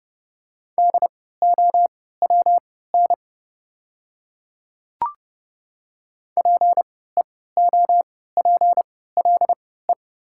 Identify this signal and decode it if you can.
Morse code, heard as a keyed tone, about 30 wpm, spaced as for 17 wpm, 700 Hz. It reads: DOWN PEOPLE
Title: 3x Character Spacing - Top 200 Words - 30wpm